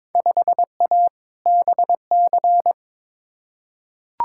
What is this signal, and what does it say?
Morse code, heard as a keyed tone, about 22 wpm, 700 Hz. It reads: 5A BC
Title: Field Day Exchanges - 22wpm